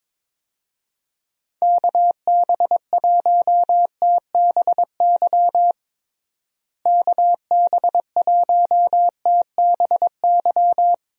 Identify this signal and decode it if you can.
Morse code, heard as a keyed tone, about 22 wpm, 700 Hz. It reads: KB1TBY KB1TBY
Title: USA Call Signs - 6 Characters - Repeated 3x - 22wpm